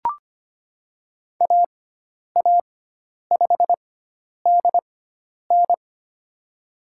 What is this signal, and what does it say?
Morse code, heard as a keyed tone, about 25 wpm, spaced as for 10 wpm, 700 Hz. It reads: AA5DN
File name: Lesson 16g - Letter c - Call Signs 25wpm